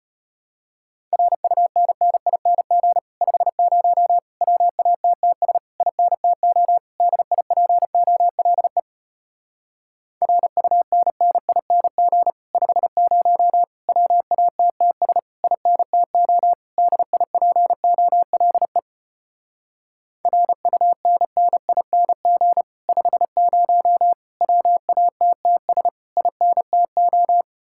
Morse code, heard as a keyed tone, about 33 wpm, 700 Hz. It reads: RUNNING 50 WATTS INTO DIPOLE RUNNING 50 WATTS INTO DIPOLE RUNNING 50 WATTS INTO